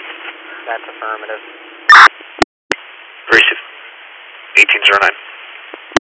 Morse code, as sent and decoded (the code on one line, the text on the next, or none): none